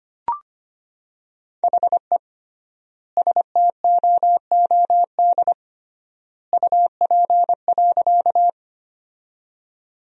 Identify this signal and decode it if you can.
Morse code, heard as a keyed tone, about 25 wpm, 700 Hz. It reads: HE STOOD UP.
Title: 3x Word-Spacing - Sentences from Top 500 Words - 25wpm